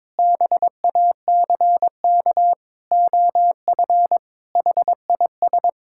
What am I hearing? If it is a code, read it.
Morse code, heard as a keyed tone, about 22 wpm, 700 Hz. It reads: BACK OF HIS
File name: practice / 22wpm / Sentences with 4-Word Phrases - 22wpm